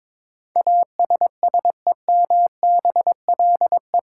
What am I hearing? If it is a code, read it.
Morse code, heard as a keyed tone, about 22 wpm, 700 Hz. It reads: ASSEMBLE
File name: practice / 22wpm / Eight Letter Words - 22wpm